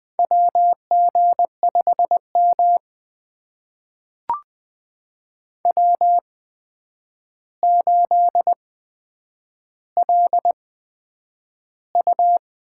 Morse code, heard as a keyed tone, about 20 wpm, spaced as for 9 wpm, 700 Hz. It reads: WG5M W8LU
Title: USA Call Signs - 4 Characters - 8x Character Spacing - 20wpm